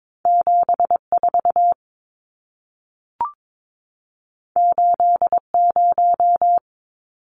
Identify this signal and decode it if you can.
Morse code, heard as a keyed tone, about 22 wpm, 700 Hz. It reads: 74 80
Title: Numbers - 2-Digit 22wpm